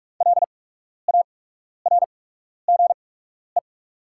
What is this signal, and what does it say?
Morse code, heard as a keyed tone, about 48 wpm, spaced as for 12 wpm, 700 Hz. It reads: LARGE
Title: 8x Character Spacing - Top 200 Words - 45wpm